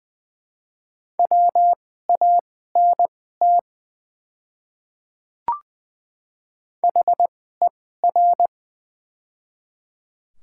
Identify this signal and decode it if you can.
Morse code, heard as a keyed tone, about 20 wpm, spaced as for 14 wpm, 700 Hz. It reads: WANT HER
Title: 2x Character Spacing - Top 200 Words - 20wpm